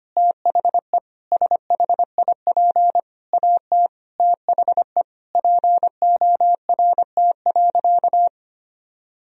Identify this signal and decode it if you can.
Morse code, heard as a keyed tone, about 25 wpm, 700 Hz. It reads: THE SHIP AT THE PORT.
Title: Sentences from Top 1000 Words - Review Part 2 - 25wpm